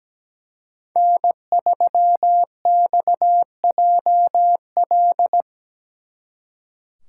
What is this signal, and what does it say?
Morse code, heard as a keyed tone, about 20 wpm, 700 Hz. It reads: N3XJL
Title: Speed Racing - USA 1x3 Call Signs 17wpm